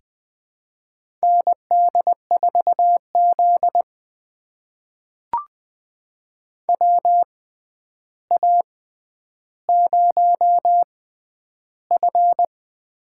Morse code, heard as a keyed tone, about 20 wpm, spaced as for 10 wpm, 700 Hz. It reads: ND4Z WA0F
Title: USA Call Signs - 4 Characters - 6x Character Spacing - 20wpm